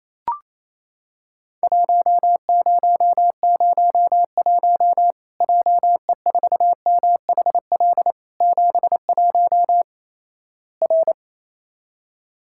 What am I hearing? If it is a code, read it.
Morse code, heard as a keyed tone, about 28 wpm, 700 Hz. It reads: 1001 JE4MHL 71 R
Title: Worked All Europe DX Contest - QTC Exchange - 28wpm